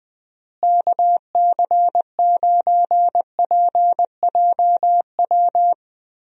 Morse code, heard as a keyed tone, about 20 wpm, 700 Hz. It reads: KC9PJW
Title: USA Call Signs - 6 Characters - Repeated 3x - 20wpm